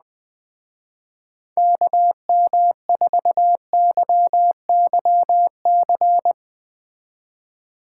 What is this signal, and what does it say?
Morse code, heard as a keyed tone, about 20 wpm, 700 Hz. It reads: KM4YYC
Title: Speed Racing - USA Call Signs 17wpm